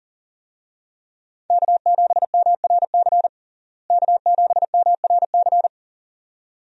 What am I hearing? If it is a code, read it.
Morse code, heard as a keyed tone, about 40 wpm, 700 Hz. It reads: K7MRC K7MRC
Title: USA Call Signs - 5 Characters - Repeated 2x - 40wpm